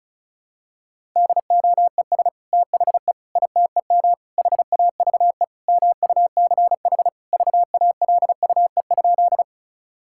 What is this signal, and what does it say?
Morse code, heard as a keyed tone, about 36 wpm, 700 Hz. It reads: DOES THE ITEM HAVE MUCH VALUE?
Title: Sentences from Top 1000 Words - Review Part 6 - 35wpm